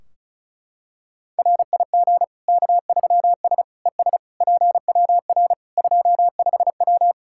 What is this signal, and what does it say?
Morse code, heard as a keyed tone, about 34 wpm, 700 Hz. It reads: RIG K3S ES PWR 25W
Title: Practice QSO - Exchange 2 - Info - 35wpm